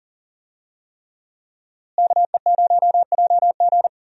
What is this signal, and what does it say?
Morse code, heard as a keyed tone, about 37 wpm, 700 Hz. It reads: KE0JG
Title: USA Call Signs - 5 Characters - Repeated 3x - 40wpm